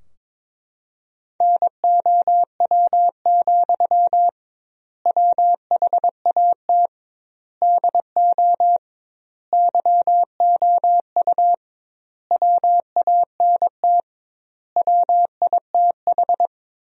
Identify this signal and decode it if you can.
Morse code, heard as a keyed tone, about 21 wpm, 700 Hz. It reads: NOW, WHAT DO YOU WANT WITH
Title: 2x Word-Spacing - Sentences from Top 200 Words - 22wpm